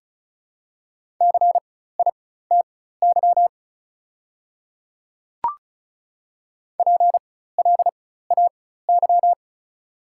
Morse code, heard as a keyed tone, about 36 wpm, spaced as for 16 wpm, 700 Hz. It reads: CITY PLAY
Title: 4x Character Spacing - Top 300 Words - 35wpm